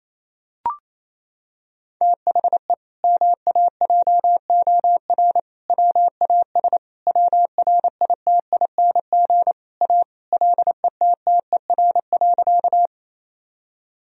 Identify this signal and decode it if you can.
Morse code, heard as a keyed tone, about 28 wpm, 700 Hz. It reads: THE MAJOR WAS WRITING A LETTER.
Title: Sentences from Top 1000 Words - Review Part 4 - 28wpm